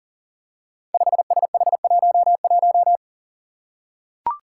Morse code, heard as a keyed tone, about 40 wpm, 700 Hz. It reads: 5SH11
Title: Five Challenging Letter-Number Combinations 40wpm